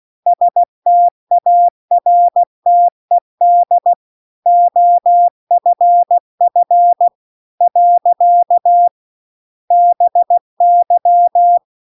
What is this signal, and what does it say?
Morse code, heard as a keyed tone, about 16 wpm, 700 Hz. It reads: STARTED OFF . BY
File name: CW-The-Big-Twoheaded-River-2-16WPM